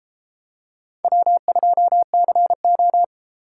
Morse code, heard as a keyed tone, about 31 wpm, 700 Hz. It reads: W2CO